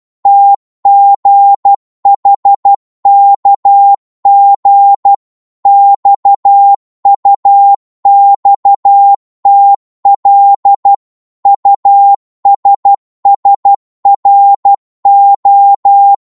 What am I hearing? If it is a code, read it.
Morse code, heard as a keyed tone, about 12 wpm, 800 Hz. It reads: TGHKG XUXTL USSRO
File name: Morse / ltr-12wpm